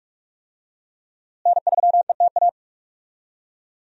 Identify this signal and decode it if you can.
Morse code, heard as a keyed tone, about 46 wpm, 700 Hz. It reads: N3ETA
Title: USA Call Signs - 5 Characters - Repeated 2x - 45wpm